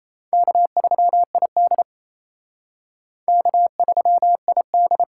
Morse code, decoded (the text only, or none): K3ID K3ID